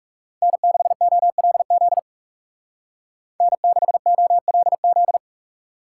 Morse code, heard as a keyed tone, about 37 wpm, 700 Hz. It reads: N6OLZ N6OLZ
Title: Speed Racing - USA 1x3 Call Signs 30wpm